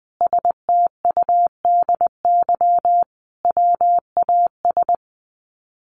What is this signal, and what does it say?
Morse code, heard as a keyed tone, about 20 wpm, 700 Hz. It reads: STUDY WAS
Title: Sets of 5 Words 20wpm